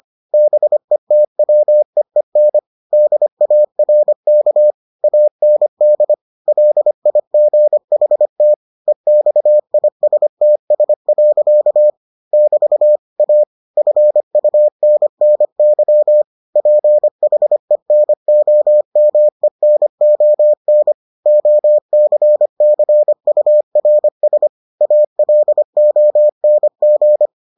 Morse code, heard as a keyed tone, about 25 wpm, 600 Hz. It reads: BETWEEN DARK AND LIGHT EXISTS. <BT> A FUNNY PHENOMENON OCCURS ALONG